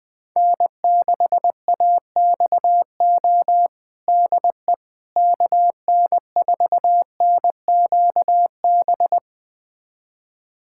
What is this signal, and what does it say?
Morse code, heard as a keyed tone, about 20 wpm, 700 Hz. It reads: N6AXO DE KN4NQB
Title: USA Callsign DE Callsign 20wpm